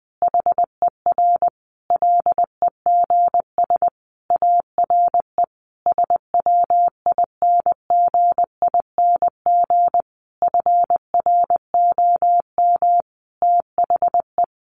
Morse code, heard as a keyed tone, about 20 wpm, 700 Hz. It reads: HER LEGS ARE SWINGING FROM THE
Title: Easy Phrases 20wpm